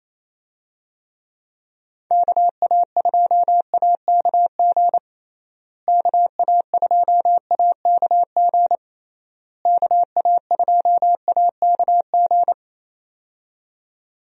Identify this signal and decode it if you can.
Morse code, heard as a keyed tone, about 28 wpm, 700 Hz. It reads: KA2AKG KA2AKG KA2AKG
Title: USA Call Signs - 6 Characters - Repeated 3x - 28wpm